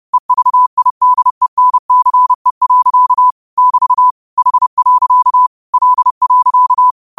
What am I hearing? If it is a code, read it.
Morse code, heard as a keyed tone, about 30 wpm, 1000 Hz. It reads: EVIDENCE. <BT> H. L.